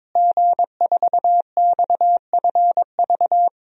Morse code, heard as a keyed tone, about 22 wpm, 700 Hz. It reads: G4XFV
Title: DXCC-100 Call Signs - 5 Characters - Repeated 3x - 22wpm